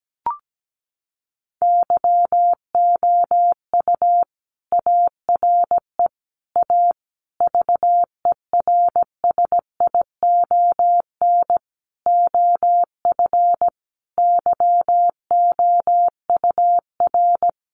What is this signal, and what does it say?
Morse code, heard as a keyed tone, about 17 wpm, 700 Hz. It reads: YOU ARE A VERSION OF YOUR
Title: Sentences from Top 1400 Words 17wpm